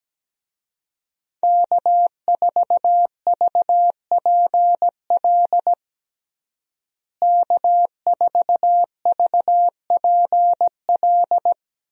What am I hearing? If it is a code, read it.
Morse code, heard as a keyed tone, about 17 wpm, 700 Hz. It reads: K4VPL K4VPL
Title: USA Call Signs - 5 Characters - Repeated 3x - 17wpm